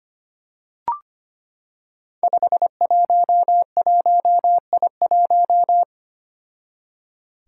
Morse code, heard as a keyed tone, about 25 wpm, 700 Hz. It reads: 511I1
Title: Five Challenging Letter-Number Combinations 25wpm